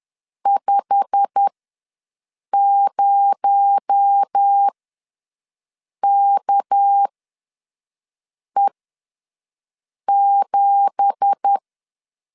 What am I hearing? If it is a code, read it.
Morse code, heard as a keyed tone, about 10 wpm, spaced as for 5 wpm, 800 Hz. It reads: 50KE7
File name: LEZIONI / 9 2867 ws 10wpm 4215